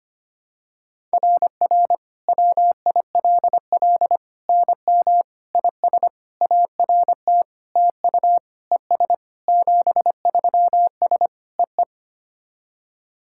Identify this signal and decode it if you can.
Morse code, heard as a keyed tone, about 25 wpm, 700 Hz. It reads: RR WILL NM IS ART TU ES 73S EE
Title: SOTA QSO Practice 25wpm